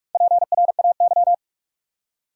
Morse code, heard as a keyed tone, about 46 wpm, 700 Hz. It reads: PRAY
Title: Four Letter Words - 45wpm